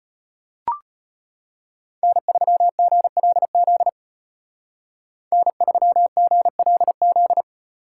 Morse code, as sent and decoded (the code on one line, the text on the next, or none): -. ...-- --. .-.. --.. / -. ...-- --. .-.. --..
N3GLZ N3GLZ